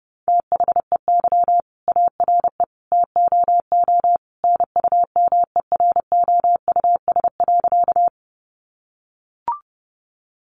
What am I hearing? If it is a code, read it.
Morse code, heard as a keyed tone, about 30 wpm, 700 Hz. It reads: THEY ARE TOO NUMEROUS.
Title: Sentences from Top 1400 Words 30wpm